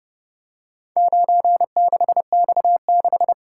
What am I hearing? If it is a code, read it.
Morse code, heard as a keyed tone, about 30 wpm, 700 Hz. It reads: 96X6